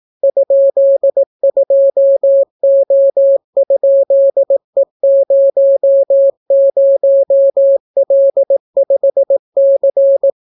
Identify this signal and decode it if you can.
Morse code, heard as a keyed tone, about 18 wpm, 550 Hz. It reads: ?2O?E00L5C